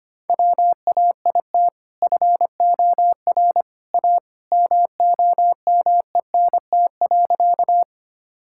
Morse code, heard as a keyed tone, about 25 wpm, 700 Hz. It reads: WAIT FOR A MOMENT.
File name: practice / 25wpm / Sentences with 2-Word Phrases - 25wpm